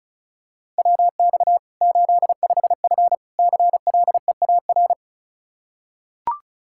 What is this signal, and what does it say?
Morse code, heard as a keyed tone, about 34 wpm, 700 Hz. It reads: WX 85F CLEAR
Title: Practice QSO - Exchange 2 - Info - 35wpm